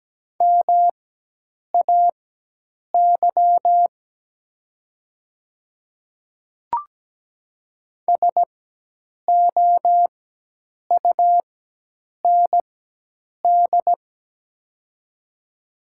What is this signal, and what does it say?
Morse code, heard as a keyed tone, about 17 wpm, spaced as for 8 wpm, 700 Hz. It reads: MAY SOUND
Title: 4x Character Spacing - Top 100 Words - 17wpm